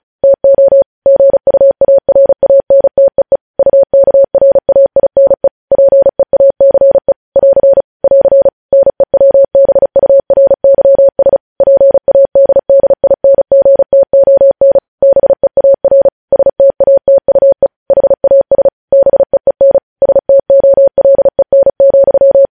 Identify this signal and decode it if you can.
Morse code, heard as a keyed tone, about 36 wpm, 550 Hz. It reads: TO GUARANTEE UKRAINE PEACE <AR> <AR> NEWBURYS PADDINGTON BEAR STATUE HAS BEEN STOLEN,
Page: News Headlines in Morse Code at 35 WPM